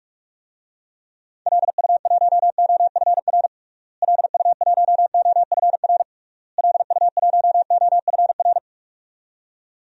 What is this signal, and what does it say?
Morse code, heard as a keyed tone, about 41 wpm, 700 Hz. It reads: LU1OFR LU1OFR LU1OFR
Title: DXCC-100 Call Signs - 6 Characters - Repeated 3x - 45wpm